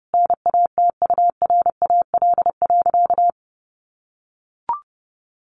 Morse code, heard as a keyed tone, about 30 wpm, 700 Hz. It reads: NATURAL.
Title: Sentences from Top 700 Words 30wpm